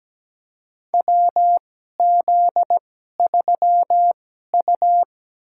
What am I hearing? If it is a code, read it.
Morse code, heard as a keyed tone, about 17 wpm, spaced as for 12 wpm, 700 Hz. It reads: WZ3U
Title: USA Call Signs - 4 Characters - 2x Character Spacing - 17wpm